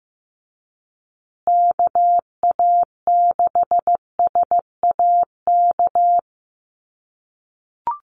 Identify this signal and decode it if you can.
Morse code, heard as a keyed tone, about 15 wpm, 700 Hz. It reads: KA6SAK
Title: USA Call Signs Repeated 2x 15wpm